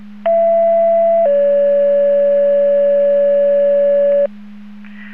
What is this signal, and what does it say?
Two-tone page, heard as a one-second tone, then a three-second tone, about 670 then 570 Hz. It is heard from 0.3 to 4.3 s.